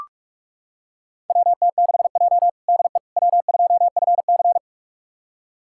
Morse code, heard as a keyed tone, about 41 wpm, 700 Hz. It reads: WT6J DE W2FC
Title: USA Callsign DE Callsign - 4 Characters - 45wpm